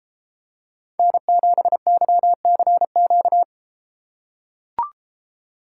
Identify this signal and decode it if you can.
Morse code, heard as a keyed tone, about 32 wpm, 700 Hz. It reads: N7YCQ